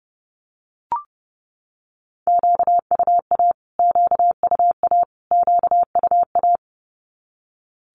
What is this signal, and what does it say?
Morse code, heard as a keyed tone, about 30 wpm, 700 Hz. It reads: QUA QUA QUA
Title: Q-Codes Repeated 3x 30wpm